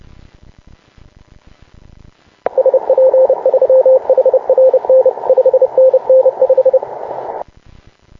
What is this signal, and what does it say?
Morse code, heard as a keyed tone, about 30 wpm, 520 Hz. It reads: SP3HRN 5NN 5